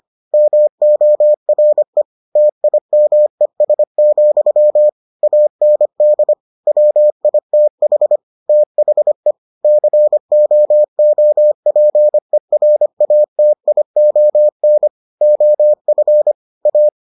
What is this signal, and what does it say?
Morse code, heard as a keyed tone, about 25 wpm, 600 Hz. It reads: MORE TIMES, AND WITH THE COOPERATION OF A